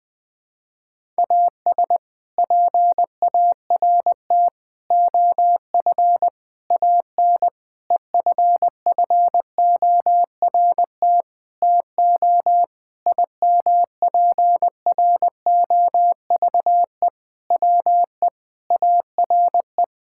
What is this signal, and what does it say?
Morse code, heard as a keyed tone, about 20 wpm, 700 Hz. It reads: AS PART OF AN EFFORT TO IMPROVE WE ARE
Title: Sentences with 6-Word Phrases - 20wpm